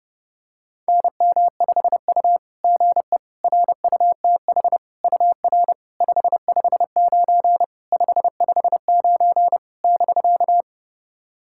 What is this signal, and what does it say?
Morse code, heard as a keyed tone, about 30 wpm, 690 Hz. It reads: NM5U GE RUTH UR 559 559 <BK>
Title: SOTA QSO Practice 30wpm